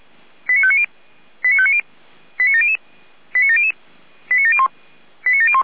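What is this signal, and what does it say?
Five-tone selective call (ZVEI), heard as quick runs of five tones, tones of about 70 ms. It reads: 89590 89590 89700 89700 89701 89701
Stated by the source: ZVEI five-tone selective call received on an SDR.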